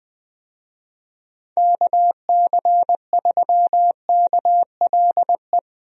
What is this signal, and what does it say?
Morse code, heard as a keyed tone, about 21 wpm, 700 Hz. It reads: KC3KLE
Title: USA Call Signs - 6 Characters - Repeated 3x - 20wpm